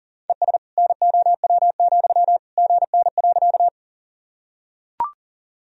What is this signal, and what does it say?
Morse code, heard as a keyed tone, about 40 wpm, 700 Hz. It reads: ES NOW, GN.
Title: Sentences from Top 300 Words and Abbreviations 40wpm